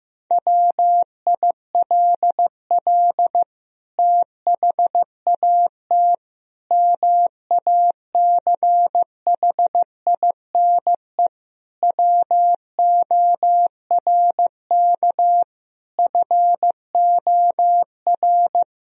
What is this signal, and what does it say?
Morse code, heard as a keyed tone, about 15 wpm, 700 Hz. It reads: WILL THAT MACHINE WORK FOR
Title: Sentences from Top 1000 Words - Review Part 2 - 15wpm